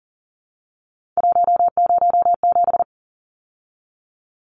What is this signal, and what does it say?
Morse code, heard as a keyed tone, about 40 wpm, 700 Hz. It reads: 107
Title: Numbers - 3-Digit 40wpm